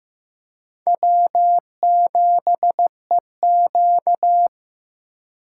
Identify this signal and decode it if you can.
Morse code, heard as a keyed tone, about 16 wpm, 700 Hz. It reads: W7EQ
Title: USA Call Signs - 4 Characters - 3x Character Spacing - 15wpm